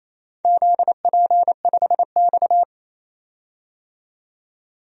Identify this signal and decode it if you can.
Morse code, heard as a keyed tone, about 28 wpm, 700 Hz. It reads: ZP5X
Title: DXCC-100 Call Signs - 4 Characters - 28wpm